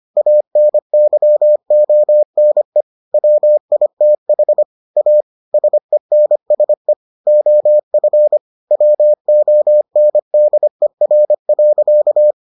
Morse code, heard as a keyed tone, about 25 wpm, 600 Hz. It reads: ANYONE WITH A SENSE OF WONDER.